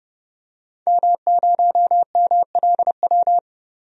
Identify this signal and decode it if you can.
Morse code, heard as a keyed tone, about 29 wpm, 700 Hz. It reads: M0MLW
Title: DXCC-100 Call Signs - 5 Characters - 30wpm